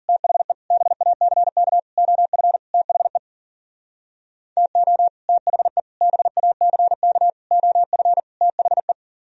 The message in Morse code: THE BACK OF THE TO THE BACK OF THE

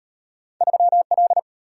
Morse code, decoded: 3L

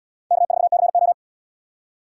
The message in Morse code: K4CO

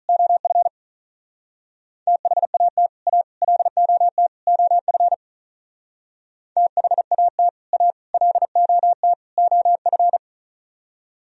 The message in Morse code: OF THAT A LOT OF THAT A LOT OF

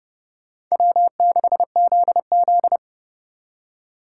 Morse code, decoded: W6ZZ

